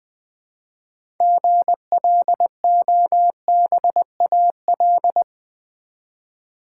Morse code, heard as a keyed tone, about 20 wpm, 700 Hz. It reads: GLOBAL
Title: 6 Letter Words - First 3 Letters Given - 20wpm